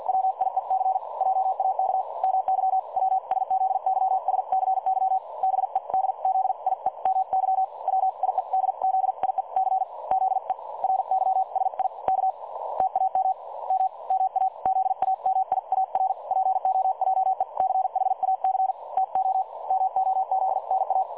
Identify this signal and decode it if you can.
Morse code, heard as a keyed tone, about 50 wpm, spaced as for 43 wpm, 760 Hz. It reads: WIEC JAK MY WSZYSCY BEDZIEMY WIEDZIEC ZE RYSIK TAM M MA ZAMIAR POJECHAC TO MOZNA
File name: dlazaawansowanych / nagraniazpasma / websdr recording 3565.0kHz